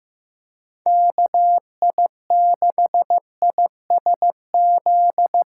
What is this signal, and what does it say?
Morse code, heard as a keyed tone, about 15 wpm, 700 Hz. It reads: KI6ISZ
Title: USA Call Signs - 6 Characters - 4x Character Spacing - 15wpm